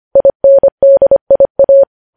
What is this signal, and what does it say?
Morse code, heard as a keyed tone, about 25 wpm, 550 Hz. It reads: INDIA
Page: news headlines in morse code at 25 wpm